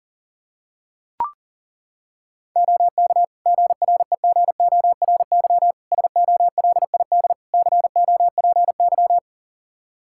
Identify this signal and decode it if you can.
Morse code, heard as a keyed tone, about 40 wpm, 700 Hz. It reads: OK GREGORY SOLID COPY
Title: Practice QSO - Exchange 3 - More Info - 40wpm